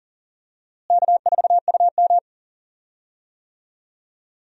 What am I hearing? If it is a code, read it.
Morse code, heard as a keyed tone, about 37 wpm, 700 Hz. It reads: K4UM
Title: USA Call Signs - 4 Characters - Repeated 2x - 40wpm